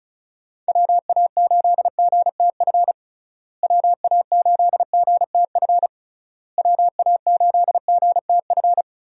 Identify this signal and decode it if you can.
Morse code, heard as a keyed tone, about 36 wpm, 700 Hz. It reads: WA8GTF WA8GTF WA8GTF
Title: USA Call Signs - 6 Characters - Repeated 3x - 35wpm